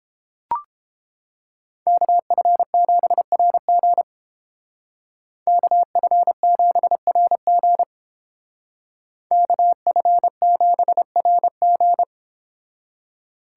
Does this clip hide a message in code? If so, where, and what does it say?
Yes: Morse code from 1.9 to 12.0 s, heard as a keyed tone, about 31 wpm, 700 Hz. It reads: KF7RG KF7RG KF7RG